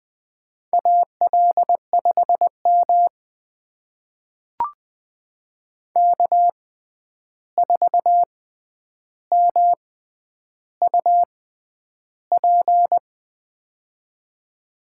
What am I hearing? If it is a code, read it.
Morse code, heard as a keyed tone, about 20 wpm, spaced as for 10 wpm, 700 Hz. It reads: AL5M K4MUP
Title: USA Call Signs - 6x Character Spacing - 20wpm